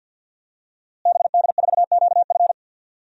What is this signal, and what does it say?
Morse code, heard as a keyed tone, about 52 wpm, 700 Hz. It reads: BD4QF
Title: DXCC-100 Call Signs - 5 Characters - Repeated 3x - 50wpm